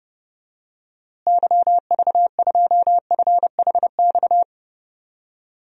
Morse code, heard as a keyed tone, about 30 wpm, 700 Hz. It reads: YV2FHX
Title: DXCC-100 Call Signs - 6 Characters - Repeated 2x - 30wpm